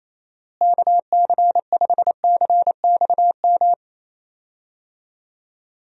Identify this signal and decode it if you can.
Morse code, heard as a keyed tone, about 28 wpm, 700 Hz. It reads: KC5CXM
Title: USA Call Signs - 6 Characters - Repeated 3x - 28wpm